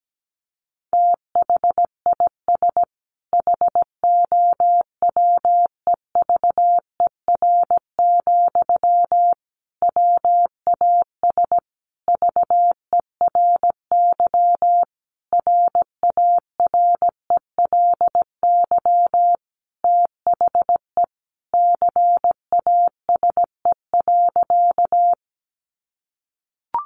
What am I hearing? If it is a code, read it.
Morse code, heard as a keyed tone, about 17 wpm, 700 Hz. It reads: THIS HOWEVER, WAS VERY RARELY THE CASE.
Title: Sentences from Top 1400 Words 17wpm